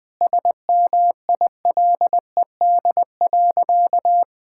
Morse code, heard as a keyed tone, about 20 wpm, 700 Hz. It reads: SMILED.
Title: Sentences with 4-Word Phrases - Spoken Beginning - 20wpm